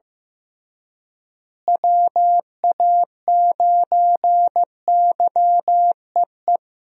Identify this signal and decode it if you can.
Morse code, heard as a keyed tone, about 15 wpm, 700 Hz. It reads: WA9YEE
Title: USA Call Signs - 6 Characters - Repeated 3x - 15wpm